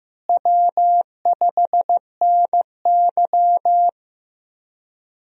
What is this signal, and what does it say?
Morse code, heard as a keyed tone, about 15 wpm, 700 Hz. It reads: W5NY
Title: USA Call Signs - 4 Characters - Repeated 3x - 15wpm